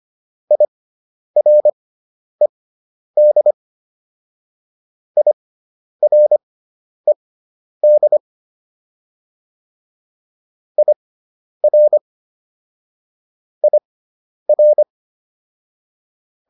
Morse code, heard as a keyed tone, about 25 wpm, spaced as for 10 wpm, 600 Hz. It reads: IRED IRED IR IR